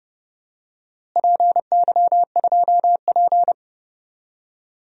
Morse code, heard as a keyed tone, about 30 wpm, 700 Hz. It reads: PY2P
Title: DXCC-100 Call Signs - 4 Characters - Repeated 3x - 30wpm